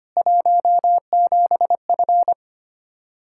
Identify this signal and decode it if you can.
Morse code, heard as a keyed tone, about 25 wpm, 700 Hz. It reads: 17F
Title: K1USN SST QSO Practice 25wpm